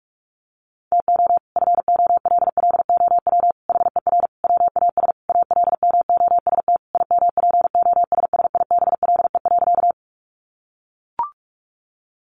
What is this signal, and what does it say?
Morse code, heard as a keyed tone, about 44 wpm, 700 Hz. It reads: TO FOLLOW HER WAS ALMOST IMPOSSIBLE.
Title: Sentences from Top 1100 Words 45wpm